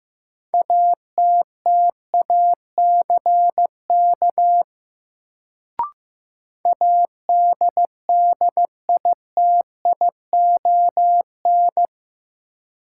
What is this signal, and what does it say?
Morse code, heard as a keyed tone, about 15 wpm, 700 Hz. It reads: ATTACK ADDITION
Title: Sentences from Top 1500 Words - New Word Drill Alt - 15wpm